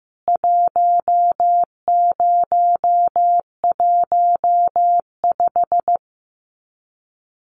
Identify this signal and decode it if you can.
Morse code, heard as a keyed tone, about 15 wpm, 700 Hz. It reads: 1015
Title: CWT Exchange 15wpm